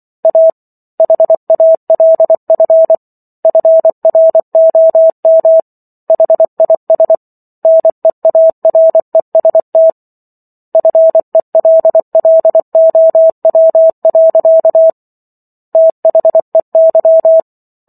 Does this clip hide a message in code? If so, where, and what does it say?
Yes: Morse code from 0.2 to 17.4 s, heard as a keyed tone, about 24 wpm, 650 Hz. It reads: A HALF FROM HIS NEAREST FELLOW. THEY